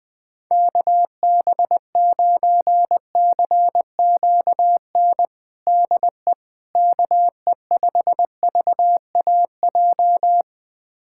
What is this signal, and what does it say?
Morse code, heard as a keyed tone, about 21 wpm, 700 Hz. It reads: KB9CQN DE KE5VAJ
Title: USA Callsign DE Callsign - 6 Characters - 20wpm